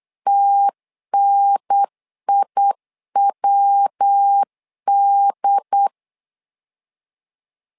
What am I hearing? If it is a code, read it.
Morse code, heard as a keyed tone, about 8 wpm, 800 Hz. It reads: TNIWD
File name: LEZIONI / 3 MWRO / 3 MW 8wpm 2501